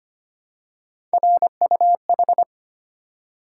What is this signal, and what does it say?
Morse code, heard as a keyed tone, about 25 wpm, 700 Hz. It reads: RUH
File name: Three Letter-Number Combinations - 25wpm